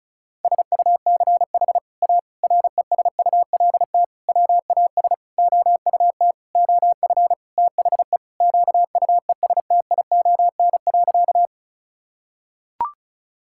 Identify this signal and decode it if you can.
Morse code, heard as a keyed tone, about 36 wpm, 700 Hz. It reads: SUCH A RESULT WAS OUT OF THE QUESTION.
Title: Sentences from Top 1000 Words - Review Part 5 - 35wpm